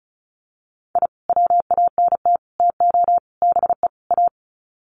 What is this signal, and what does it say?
Morse code, heard as a keyed tone, about 34 wpm, 700 Hz. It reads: I WANT TO BE A
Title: Sets of 5 Words Encore 35wpm